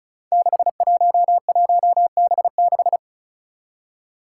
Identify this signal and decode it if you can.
Morse code, heard as a keyed tone, about 34 wpm, 700 Hz. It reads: 611B6
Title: Five Challenging Letter-Number Combinations 35wpm